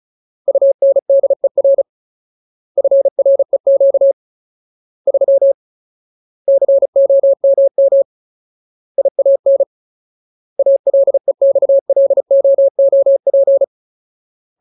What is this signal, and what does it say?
Morse code, heard as a keyed tone, about 34 wpm, 550 Hz. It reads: UNDER FREQ 3 COMM IAN ALEXLOOP
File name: Ultimate Word List / 35WPM Ultimate